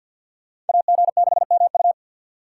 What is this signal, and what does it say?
Morse code, heard as a keyed tone, about 46 wpm, 700 Hz. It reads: AG6MU
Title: USA Call Signs - 5 Characters - 4x Character Spacing - 50wpm